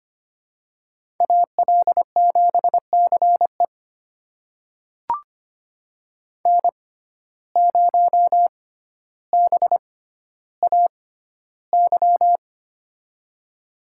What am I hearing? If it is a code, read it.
Morse code, heard as a keyed tone, about 25 wpm, spaced as for 13 wpm, 700 Hz. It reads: AL7CE N0BAY
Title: USA Call Signs - 5 Characters - 6x Character Spacing - 25wpm